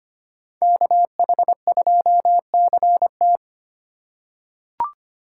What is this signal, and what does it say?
Morse code, heard as a keyed tone, about 25 wpm, 700 Hz. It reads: KH2CT
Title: USA Call Signs - 5 Characters - 2x Character Spacing - 25wpm